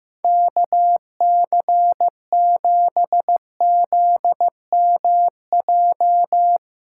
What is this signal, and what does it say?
Morse code, heard as a keyed tone, about 15 wpm, 700 Hz. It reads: KC7ZMJ
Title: USA Callsign DE Callsign - 6 Characters - 15wpm